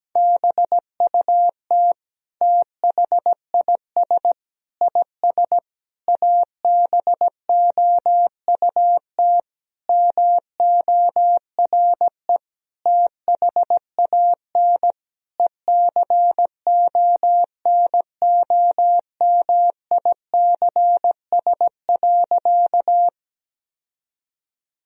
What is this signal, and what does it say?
Morse code, heard as a keyed tone, about 17 wpm, 700 Hz. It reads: BUT THIS IS ABOUT MORE THAN ECONOMICS.
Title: Sentences from Top 1000 Words - Review Part 3 - 17wpm